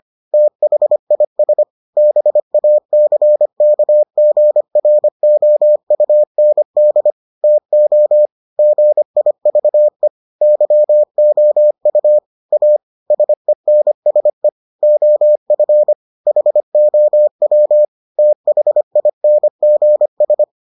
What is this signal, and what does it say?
Morse code, heard as a keyed tone, about 25 wpm, 600 Hz. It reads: THIS BACKGROUND TO GIVE YOU A SENSE OF HOW THINGS